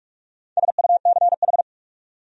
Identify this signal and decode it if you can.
Morse code, heard as a keyed tone, about 59 wpm, 700 Hz. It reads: SUCH